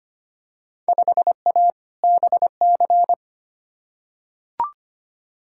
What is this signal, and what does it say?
Morse code, heard as a keyed tone, about 25 wpm, 700 Hz. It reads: 5A BC
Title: Field Day Exchanges - 25wpm